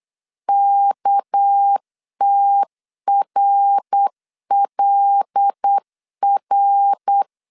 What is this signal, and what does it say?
Morse code, heard as a keyed tone, about 8 wpm, 800 Hz. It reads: KTRLR